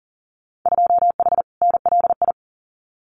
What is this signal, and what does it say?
Morse code, heard as a keyed tone, about 40 wpm, 700 Hz. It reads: 2H NLI
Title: Winter Field Day Exchanges Encore 40wpm